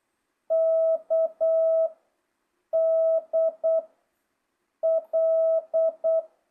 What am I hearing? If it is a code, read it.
Morse code, heard as a keyed tone, about 8 wpm, 660 Hz. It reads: K D L
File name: jh8cbh / 2025cw